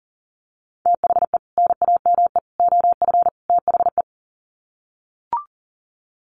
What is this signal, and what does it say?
Morse code, heard as a keyed tone, about 40 wpm, 700 Hz. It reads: THE NAME OF THE
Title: Sets of 4 Words 40wpm